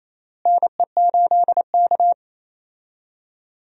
Morse code, heard as a keyed tone, about 28 wpm, 700 Hz. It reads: NE8K
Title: USA Call Signs - 4 Characters - Repeated 3x - 28wpm